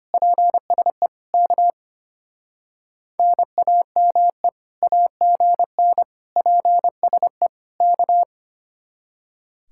Morse code, heard as a keyed tone, about 32 wpm, 700 Hz. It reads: PSE K NAME AGN PSE K